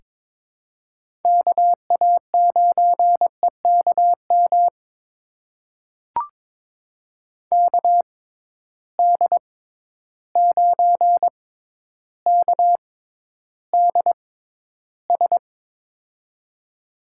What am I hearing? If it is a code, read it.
Morse code, heard as a keyed tone, about 22 wpm, spaced as for 11 wpm, 700 Hz. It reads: KA9EKM KD9KDS
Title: USA Call Signs - 6 Characters - 6x Character Spacing - 22wpm